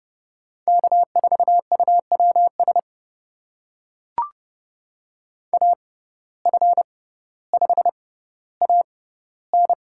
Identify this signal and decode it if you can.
Morse code, heard as a keyed tone, about 30 wpm, spaced as for 15 wpm, 700 Hz. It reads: K4UWS AF5AN
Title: USA Call Signs - 5 Characters - 6x Character Spacing - 30wpm